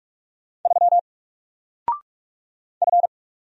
Morse code, heard as a keyed tone, about 44 wpm, 700 Hz. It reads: 3 F
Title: Single Letter-Number 45wpm